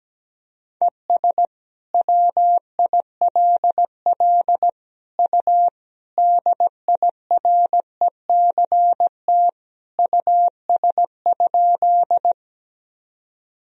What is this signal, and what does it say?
Morse code, heard as a keyed tone, about 17 wpm, 700 Hz. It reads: ES WILL U DIRECT US?
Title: Sentences from Top 400 Words and Abbreviations 17wpm